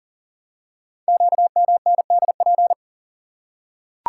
Morse code, heard as a keyed tone, about 42 wpm, 690 Hz. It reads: QMNDP
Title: Code Groups - 5 Letters - 40wpm